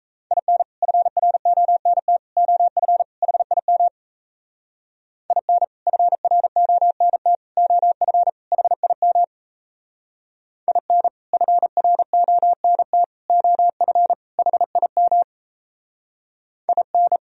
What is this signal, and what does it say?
Morse code, heard as a keyed tone, about 35 wpm, 700 Hz. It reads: IN FRONT OF HIM IN FRONT OF HIM IN FRONT OF HIM IN